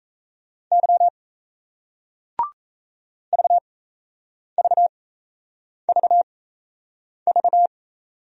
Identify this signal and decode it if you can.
Morse code, heard as a keyed tone, about 38 wpm, spaced as for 8 wpm, 700 Hz. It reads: Y VVVV